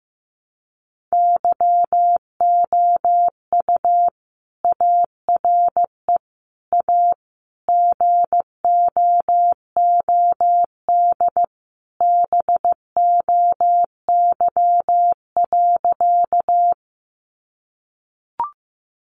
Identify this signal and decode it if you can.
Morse code, heard as a keyed tone, about 15 wpm, 700 Hz. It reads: YOU ARE A GOOD BOY.
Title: Sentences from Top 200 Words 15wpm